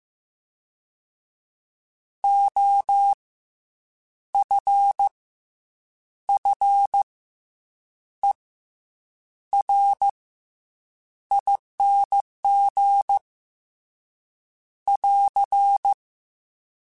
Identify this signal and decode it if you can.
Morse code, heard as a keyed tone, about 15 wpm, 800 Hz. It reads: O F F E R ING <AR>
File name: Special INGs / Special INGs-mp3s / SlowToFaster ing @13